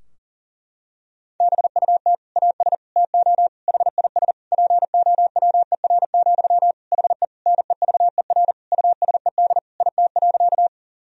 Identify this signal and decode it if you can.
Morse code, heard as a keyed tone, about 40 wpm, 700 Hz. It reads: BUT AS TO HIS POWER, HE NEVER USED IT.
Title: Sentences from Top 1000 Words - Review Part 2 - 40wpm